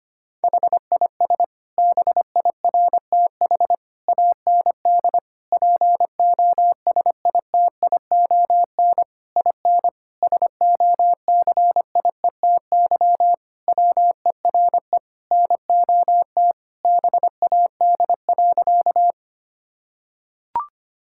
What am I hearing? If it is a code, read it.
Morse code, heard as a keyed tone, about 25 wpm, 700 Hz. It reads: HIS BIRTH AND POSITION IN SOCIETY WERE NOT BAD.
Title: Sentences from Top 1000 Words - Review Part 5 - 25wpm